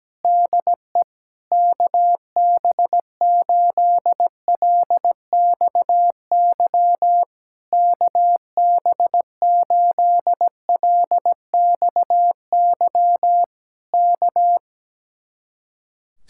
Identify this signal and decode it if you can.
Morse code, heard as a keyed tone, about 17 wpm, 700 Hz. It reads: DE KB8LXY KB8LXY K
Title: Speed-Racing Common QSO Phrases Encore - 17wpm